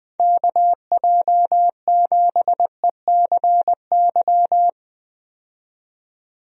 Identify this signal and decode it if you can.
Morse code, heard as a keyed tone, about 20 wpm, 700 Hz. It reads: KJ7ECY
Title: USA Call Signs - 6 Characters - 20wpm